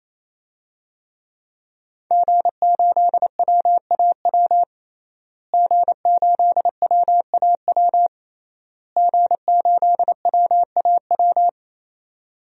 Morse code, heard as a keyed tone, about 28 wpm, 700 Hz. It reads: G8WAW G8WAW G8WAW
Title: DXCC-100 Call Signs - 5 Characters - Repeated 3x - 28wpm